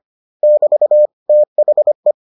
Morse code, heard as a keyed tone, about 25 wpm, 600 Hz. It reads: <BT> THE